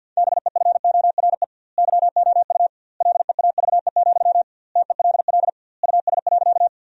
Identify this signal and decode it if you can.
Morse code, heard as a keyed tone, about 50 wpm, 700 Hz. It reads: BEFORE YOU LEAVE, TELL US.
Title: 2x Word-Spacing - Sentences from Top 400 Words - 50wpm